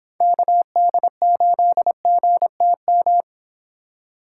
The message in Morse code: KD8GTM

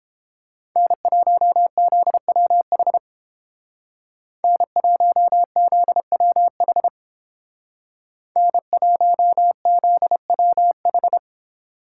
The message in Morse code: N1ZWH N1ZWH N1ZWH

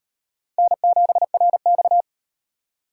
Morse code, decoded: N7RX

